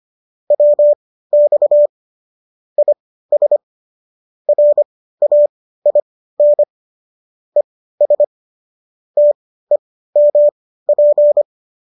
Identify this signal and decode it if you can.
Morse code, heard as a keyed tone, about 25 wpm, spaced as for 15 wpm, 590 Hz. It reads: WX IS RAIN ES TEMP